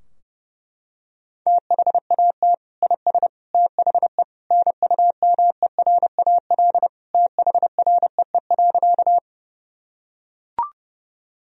Morse code, heard as a keyed tone, about 32 wpm, 700 Hz. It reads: THAT IS THE NUMERAL THREE.
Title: Sentences from Top 1000 Words - Review Part 2 - 30wpm